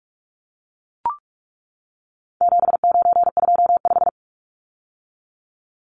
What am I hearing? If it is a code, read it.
Morse code, heard as a keyed tone, about 44 wpm, 700 Hz. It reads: 7925
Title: Numbers - 4-Digit 45wpm